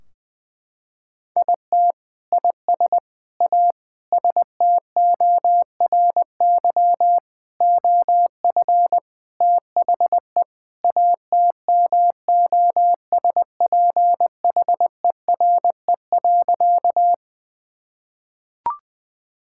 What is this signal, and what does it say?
Morse code, heard as a keyed tone, about 20 wpm, 700 Hz. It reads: IT IS A STORY OF THE ATMOSPHERE.
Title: Sentences from Top 1600 Words 20wpm